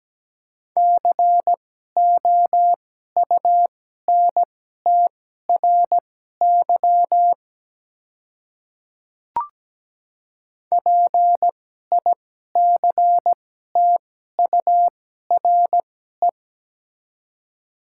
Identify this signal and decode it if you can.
Morse code, heard as a keyed tone, about 17 wpm, spaced as for 12 wpm, 700 Hz. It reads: COUNTRY PICTURE
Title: 2x Character Spacing - Top 300 Words - 17wpm